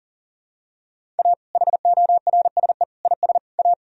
Morse code, heard as a keyed tone, about 40 wpm, 700 Hz. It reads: A HORSE IS A
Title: Sentences from Top 1000 Words - Review Part 1 - 40wpm